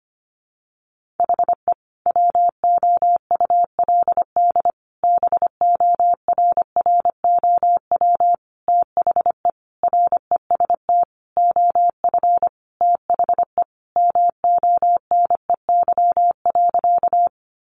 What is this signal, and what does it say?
Morse code, heard as a keyed tone, about 25 wpm, 700 Hz. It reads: HE WOULD BORROW THE REST OF THE MONEY.
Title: Sentences from Top 1500 Words 25wpm